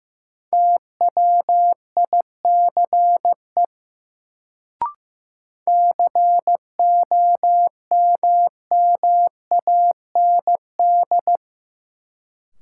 Morse code, heard as a keyed tone, about 15 wpm, 700 Hz. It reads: TWICE COMMAND
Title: Sentences from Top 1600 Words - New Word Drill Alt - 15wpm